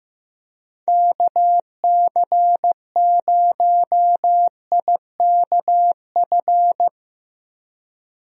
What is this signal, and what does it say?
Morse code, heard as a keyed tone, about 15 wpm, 700 Hz. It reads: KC0IKF
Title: USA Call Signs - 6 Characters - Repeated 3x - 15wpm